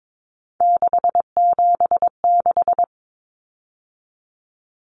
Morse code, heard as a keyed tone, about 22 wpm, 700 Hz. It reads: 676
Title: Numbers - 3-Digit 22wpm